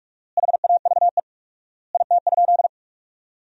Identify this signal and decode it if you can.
Morse code, heard as a keyed tone, about 46 wpm, 700 Hz. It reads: HAVE IT?
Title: 4x Word-Spacing - Sentences from Top 100 Words - 45wpm